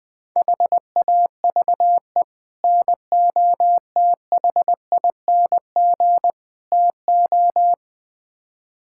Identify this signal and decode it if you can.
Morse code, heard as a keyed tone, about 20 wpm, 700 Hz. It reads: HAVE NOTHING TO
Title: Build A Phrase - 5-Word Phrases - 20wpm